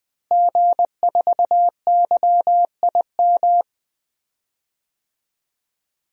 Morse code, heard as a keyed tone, about 20 wpm, 700 Hz. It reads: G4YIM
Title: DXCC-100 Call Signs - 5 Characters - Repeated 3x - 20wpm